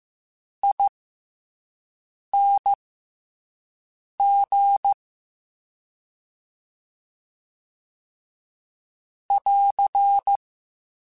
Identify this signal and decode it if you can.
Morse code, heard as a keyed tone, about 15 wpm, spaced as for 5 wpm, 800 Hz. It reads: ING <AR>
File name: Special INGs / Special INGs-mp3s / Slow ing @5